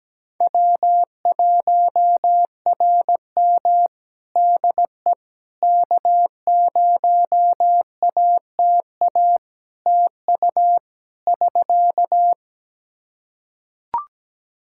Morse code, heard as a keyed tone, about 17 wpm, 700 Hz. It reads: W1RM DE K0ATA TU <SK>
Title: Practice QSO - Exchange 4 - Ending - 17wpm